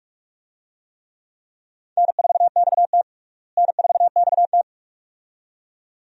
Morse code, heard as a keyed tone, about 45 wpm, 700 Hz. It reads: N4XT N4XT
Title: USA Call Signs - 4 Characters - Repeated 2x - 45wpm